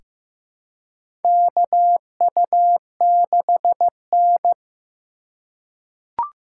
Morse code, heard as a keyed tone, about 15 wpm, 700 Hz. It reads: KU6N
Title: USA Call Signs - 4 Characters - 3x Character Spacing - 15wpm